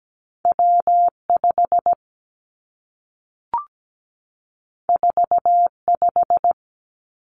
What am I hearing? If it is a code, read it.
Morse code, heard as a keyed tone, about 17 wpm, 700 Hz. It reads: W5 45
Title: Two Challenging Letter-Number Combinations 17wpm